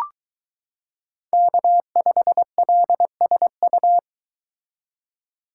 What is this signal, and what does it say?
Morse code, heard as a keyed tone, about 22 wpm, 700 Hz. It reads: K5LSU